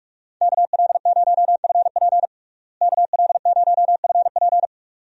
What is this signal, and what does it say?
Morse code, heard as a keyed tone, about 43 wpm, 700 Hz. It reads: KL0FP KL0FP
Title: USA Call Signs - 5 Characters - Repeated 2x - 45wpm